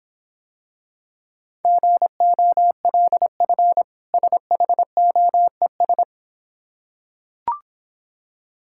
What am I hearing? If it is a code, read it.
Morse code, heard as a keyed tone, about 25 wpm, 700 Hz. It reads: GOLF SHOES
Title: Difficult Words - Speed-Racing with Hints - 20wpm